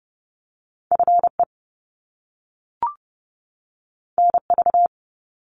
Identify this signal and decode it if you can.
Morse code, heard as a keyed tone, about 30 wpm, 700 Hz. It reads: FE NV
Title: Top Two Letter Combinations in Words 30wpm